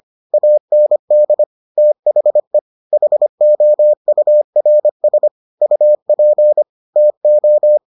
Morse code, heard as a keyed tone, about 25 wpm, 600 Hz. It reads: AND THE HOURS UP TO